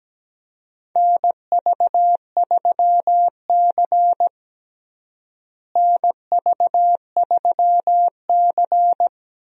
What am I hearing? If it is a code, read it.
Morse code, heard as a keyed tone, about 17 wpm, 700 Hz. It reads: NV3C NV3C
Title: USA Call Signs - 4 Characters - Repeated 2x - 17wpm